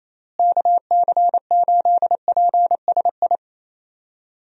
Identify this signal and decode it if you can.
Morse code, heard as a keyed tone, about 28 wpm, 700 Hz. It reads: KC8PSI
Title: USA Call Signs - 6 Characters - Repeated 3x - 28wpm